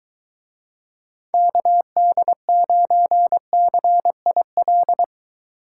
Morse code, heard as a keyed tone, about 22 wpm, 700 Hz. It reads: KD9CIL